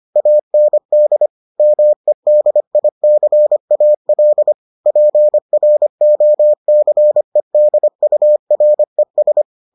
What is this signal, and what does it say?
Morse code, heard as a keyed tone, about 25 wpm, 600 Hz. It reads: AND MEDICAL PROCEDURES